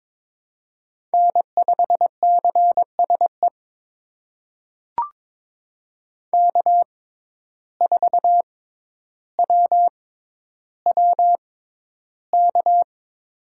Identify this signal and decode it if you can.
Morse code, heard as a keyed tone, about 22 wpm, spaced as for 11 wpm, 700 Hz. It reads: N5CSE K4WWK
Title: USA Call Signs - 5 Characters - 6x Character Spacing - 22wpm